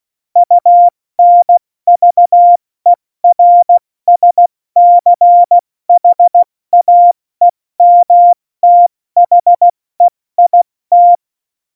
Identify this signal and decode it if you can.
Morse code, heard as a keyed tone, about 16 wpm, 700 Hz. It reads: UNVERSCHAEMTHEIT
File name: Morsekurs / Witze / Witz 11. September 2018, 70 F80